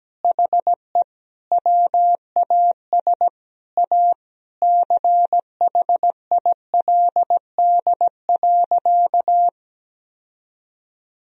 Morse code, heard as a keyed tone, about 17 wpm, 700 Hz. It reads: HE WAS A CHILD.
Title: Sentences from Top 1000 Words - Review Part 3 - 17wpm